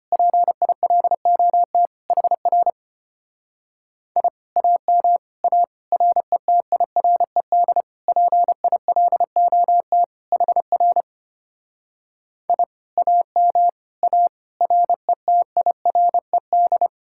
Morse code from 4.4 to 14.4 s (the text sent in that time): AM A RETIRED PILOT HR I AM A